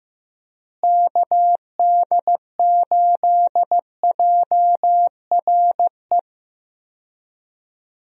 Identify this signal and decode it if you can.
Morse code, heard as a keyed tone, about 15 wpm, 700 Hz. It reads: KD8JRE
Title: USA Call Signs - 6 Characters - Repeated 3x - 15wpm